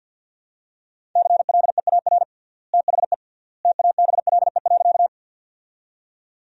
Morse code, heard as a keyed tone, about 50 wpm, 700 Hz. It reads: CLEAR THE TABLE.
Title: 3x Word-Spacing - Sentences from Top 500 Words - 50wpm